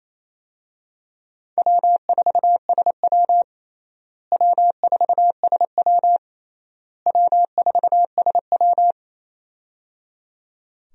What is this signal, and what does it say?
Morse code, heard as a keyed tone, about 28 wpm, 700 Hz. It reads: W4SW W4SW W4SW
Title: USA Call Signs - 4 Characters - Repeated 3x - 28wpm